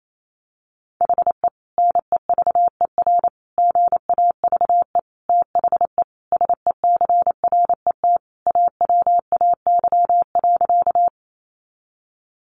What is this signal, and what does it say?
Morse code, heard as a keyed tone, about 28 wpm, 700 Hz. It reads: HE NEVER GAVE THE SECRET AWAY.
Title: Sentences from Top 1200 Words 28wpm